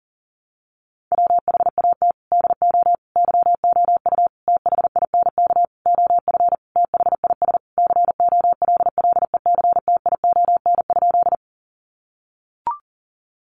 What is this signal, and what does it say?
Morse code, heard as a keyed tone, about 40 wpm, 700 Hz. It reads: WHAT DO YOU THINK OF THIS COLLECTION?
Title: Sentences from Top 1200 Words 40wpm